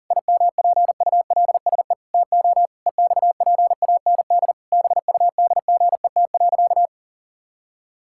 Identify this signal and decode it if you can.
Morse code, heard as a keyed tone, about 40 wpm, 700 Hz. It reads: IMPULSE TO EXPAND BUDGET.
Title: Pangram Challenge v2 - 40wpm